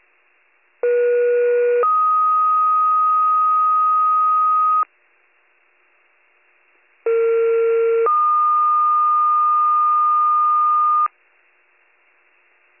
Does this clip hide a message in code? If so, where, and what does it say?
Yes: two-tone page from 0.8 to 11.1 s, heard as a one-second tone, then a three-second tone, about 480 then 1250 Hz, about 460 then 1220 Hz.